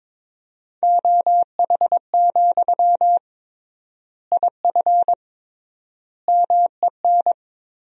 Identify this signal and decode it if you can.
Morse code, heard as a keyed tone, about 22 wpm, 700 Hz. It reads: OH, IF MEN
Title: 3x Word-Spacing - Sentences from Top 500 Words - 22wpm